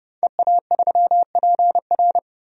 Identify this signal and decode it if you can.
Morse code, heard as a keyed tone, about 30 wpm, 690 Hz. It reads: EA3PR
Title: DXCC-100 Call Signs - 5 Characters - 30wpm